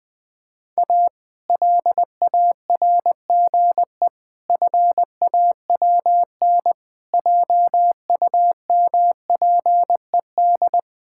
Morse code, heard as a keyed tone, about 20 wpm, 700 Hz. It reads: A LARGE FAWN JUMPED
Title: Pangram Challenge v2 - 20wpm